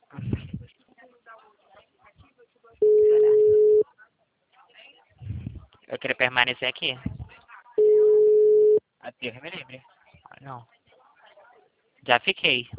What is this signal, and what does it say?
Ringback tone (Europe, 425 Hz), with speech.